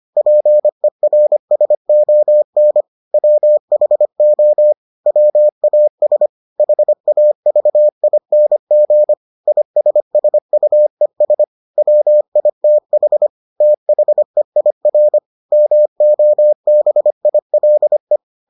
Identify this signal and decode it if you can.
Morse code, heard as a keyed tone, about 25 wpm, 600 Hz. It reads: PERSON WHO WAS HAVING ISSUES WITH THEIR MOBILE